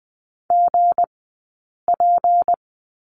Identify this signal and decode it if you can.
Morse code, heard as a keyed tone, about 20 wpm, spaced as for 8 wpm, 700 Hz. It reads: GP
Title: 165 Lesson 27d - Number 7 - Two Characters 20wpm